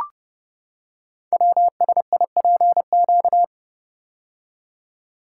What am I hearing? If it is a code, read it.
Morse code, heard as a keyed tone, about 30 wpm, 700 Hz. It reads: WSIPQ
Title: Code Groups - 5 Letters - 30wpm